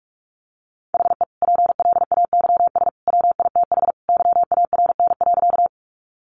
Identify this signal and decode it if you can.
Morse code, heard as a keyed tone, about 44 wpm, 700 Hz. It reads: HE PLAYS WITH YARN.
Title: Easy Phrases 45wpm